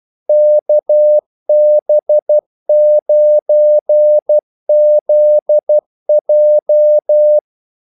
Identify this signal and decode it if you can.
Morse code, heard as a keyed tone, about 12 wpm, 600 Hz. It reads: KB9ZJ
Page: 2 ekstra ordmellemrum mellem kaldesignalerne